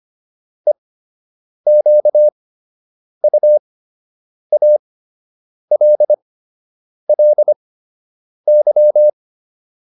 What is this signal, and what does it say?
Morse code, heard as a keyed tone, about 25 wpm, spaced as for 8 wpm, 600 Hz. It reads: EQUALLY